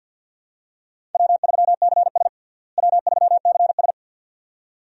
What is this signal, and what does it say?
Morse code, heard as a keyed tone, about 53 wpm, 700 Hz. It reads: W3CS W3CS
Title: USA Call Signs - 4 Characters - Repeated 2x - 50wpm